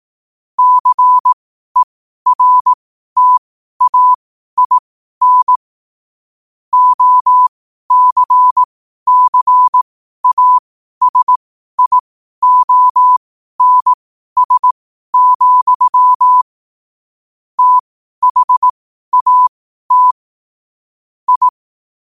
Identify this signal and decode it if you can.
Morse code, heard as a keyed tone, about 18 wpm, spaced as for 12 wpm, 1000 Hz. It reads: CERTAIN OCCASIONS, THAT I
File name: qotdfiles / 12wpm